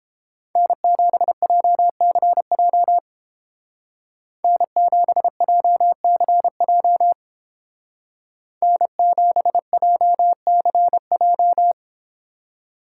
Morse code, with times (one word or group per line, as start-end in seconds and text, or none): 0.6-3.0 N7JCJ
4.4-7.1 N7JCJ
8.6-11.7 N7JCJ